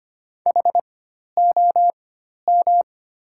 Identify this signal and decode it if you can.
Morse code, heard as a keyed tone, about 25 wpm, spaced as for 12 wpm, 700 Hz. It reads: HOM
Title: 4x Character Spacing - Top 200 Words - 25wpm